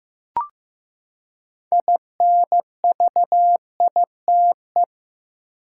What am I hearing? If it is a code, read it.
Morse code, heard as a keyed tone, about 15 wpm, 700 Hz. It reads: INVITE
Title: Six Letter Words - 15wpm